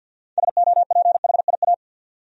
Morse code, heard as a keyed tone, about 47 wpm, 700 Hz. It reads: SOPHIA
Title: USA Common Names - 50wpm